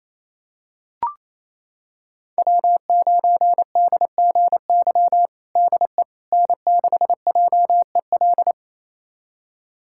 Morse code, heard as a keyed tone, about 27 wpm, 700 Hz. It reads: W9DGY DE N6JEL
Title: USA Callsign DE Callsign - 5 Characters - 28wpm